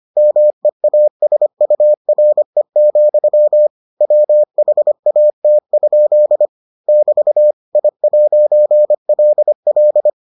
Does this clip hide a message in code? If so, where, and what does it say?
Yes: Morse code from 0.2 to 10.1 s, heard as a keyed tone, about 25 wpm, 600 Hz. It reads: MEASURE, WHAT? <BT> I'LL